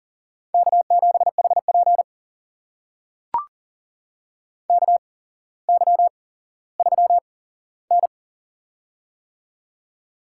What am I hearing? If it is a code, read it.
Morse code, heard as a keyed tone, about 40 wpm, spaced as for 17 wpm, 700 Hz. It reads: K7HP KY3N
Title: USA Call Signs - 4 Characters - 8x Character Spacing - 40wpm